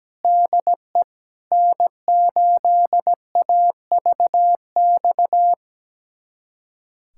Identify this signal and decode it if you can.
Morse code, heard as a keyed tone, about 17 wpm, 700 Hz. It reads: DE N8AVX
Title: USA Callsign DE Callsign 17wpm